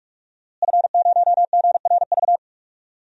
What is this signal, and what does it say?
Morse code, heard as a keyed tone, about 46 wpm, 700 Hz. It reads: F0GRV
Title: DXCC-100 Call Signs - 5 Characters - Repeated 2x - 45wpm